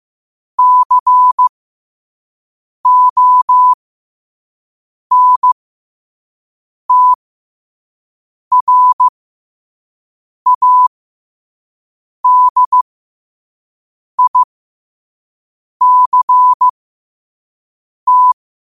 Morse code, heard as a keyed tone, about 15 wpm, spaced as for 5 wpm, 1000 Hz. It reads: CONTRADICT
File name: qotdfiles / 5wpm